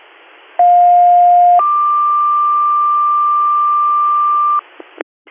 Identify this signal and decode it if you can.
Two-tone page, heard as a one-second tone, then a three-second tone, about 710 then 1180 Hz, with noise. It is heard from 0.6 to 4.6 s.